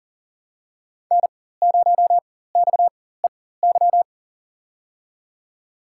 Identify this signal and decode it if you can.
Morse code, heard as a keyed tone, about 38 wpm, spaced as for 19 wpm, 690 Hz. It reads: N0XEY
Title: USA Call Signs - 4x Character Spacing - 40wpm